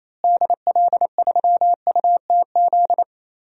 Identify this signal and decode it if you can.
Morse code, heard as a keyed tone, about 28 wpm, 700 Hz. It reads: DL3UTZ
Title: DXCC-100 Call Signs - 6 Characters - Repeated 3x - 28wpm